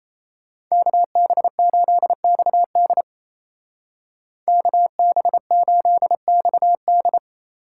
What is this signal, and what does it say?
Morse code, heard as a keyed tone, about 37 wpm, 700 Hz. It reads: KB8XD KB8XD